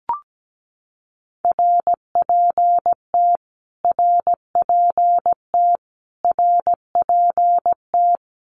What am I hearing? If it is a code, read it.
Morse code, heard as a keyed tone, about 17 wpm, 700 Hz. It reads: RPT RPT RPT
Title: Abbreviations Repeated 3x - 17wpm